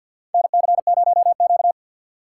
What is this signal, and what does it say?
Morse code, heard as a keyed tone, about 47 wpm, 700 Hz. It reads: NC0Q
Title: USA Call Signs - 4 Characters - 6x Character Spacing - 50wpm